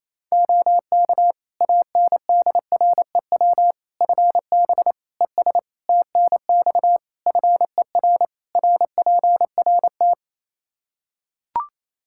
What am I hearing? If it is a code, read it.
Morse code, heard as a keyed tone, about 28 wpm, 700 Hz. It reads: OK ANDREW FB ES TNX FER RPRT
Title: Practice QSO - Exchange 2 - Info - 28wpm